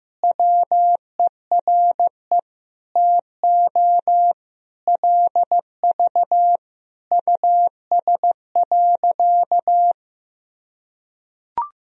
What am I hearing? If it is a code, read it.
Morse code, heard as a keyed tone, about 15 wpm, 700 Hz. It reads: WERE TO LV US.
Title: Sentences from Top 400 Words and Abbreviations 15wpm